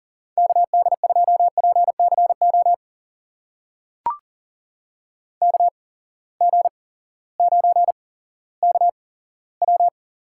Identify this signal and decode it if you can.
Morse code, heard as a keyed tone, about 40 wpm, spaced as for 18 wpm, 700 Hz. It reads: KD2PCO KG9KW
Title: USA Call Signs - 8x Character Spacing - 40wpm